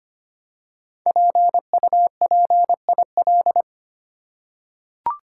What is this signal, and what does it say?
Morse code, heard as a keyed tone, about 25 wpm, 700 Hz. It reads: PUPIL
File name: 5 Letter Words - First 3 Letters Given - 25wpm